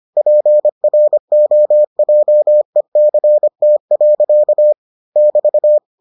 Morse code, heard as a keyed tone, about 25 wpm, 600 Hz. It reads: PROJECT. <BT>